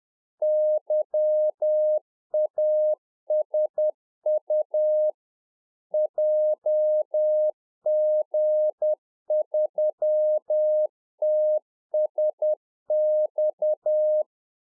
Morse code, heard as a keyed tone, about 10 wpm, 610 Hz. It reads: YASU JG3TSX
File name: 2011-04-mp3 / NEW MEMBERS 10WPM